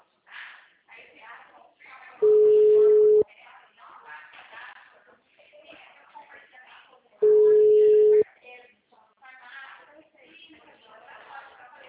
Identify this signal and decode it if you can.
Ringback tone (Europe, 425 Hz), with speech.
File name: out- -2010- - - .62853